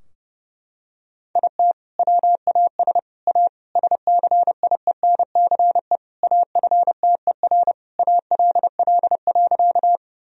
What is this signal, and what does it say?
Morse code, heard as a keyed tone, about 30 wpm, 700 Hz. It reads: IT WAS A SCIENCE AFTER ALL.
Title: Sentences from Top 1000 Words - Review Part 2 - 30wpm